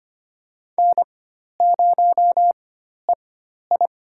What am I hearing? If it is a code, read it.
Morse code, heard as a keyed tone, about 25 wpm, spaced as for 12 wpm, 700 Hz. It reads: N0EI
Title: USA Call Signs - 4x Character Spacing - 25wpm